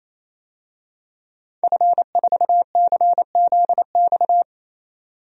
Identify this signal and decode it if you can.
Morse code, heard as a keyed tone, about 27 wpm, 700 Hz. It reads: F4CZX
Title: DXCC-100 Call Signs - 5 Characters - 28wpm